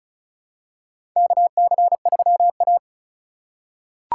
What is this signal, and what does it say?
Morse code, heard as a keyed tone, about 36 wpm, 700 Hz. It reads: KC3A
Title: USA Call Signs - 4 Characters - 2x Character Spacing - 35wpm